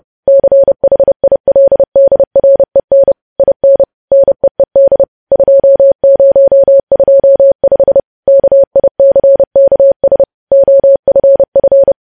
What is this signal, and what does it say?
Morse code, heard as a keyed tone, about 30 wpm, 550 Hz. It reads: CHILDREN IN NEED 2025 KICKS OFF
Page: News Headlines in Morse Code at 30 WPM